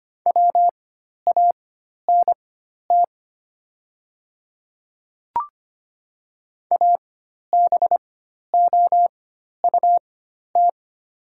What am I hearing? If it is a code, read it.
Morse code, heard as a keyed tone, about 25 wpm, spaced as for 12 wpm, 690 Hz. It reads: WANT ABOUT
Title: 4x Character Spacing - Top 200 Words - 25wpm